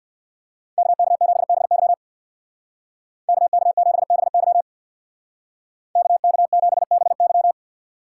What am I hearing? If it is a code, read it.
Morse code, heard as a keyed tone, about 60 wpm, 700 Hz. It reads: KK7BY KK7BY KK7BY
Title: Speed Racing - USA 2x2 Call Signs 50wpm